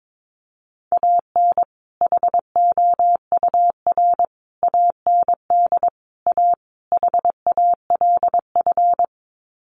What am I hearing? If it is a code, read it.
Morse code, heard as a keyed tone, about 22 wpm, 700 Hz. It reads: AN HOUR AND A HALF
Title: Sets of 5 Words 22wpm